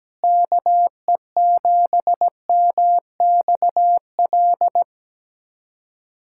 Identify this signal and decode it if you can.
Morse code, heard as a keyed tone, about 17 wpm, 700 Hz. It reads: KE7MXL
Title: USA Callsign DE Callsign 17wpm